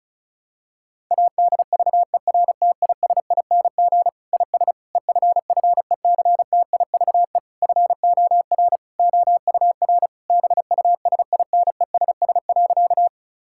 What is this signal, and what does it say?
Morse code, heard as a keyed tone, about 36 wpm, 700 Hz. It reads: ADVERTISING IS EFFECTIVE FOR OUR BUSINESS.
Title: Sentences from Top 1000 Words - Review Part 6 - 35wpm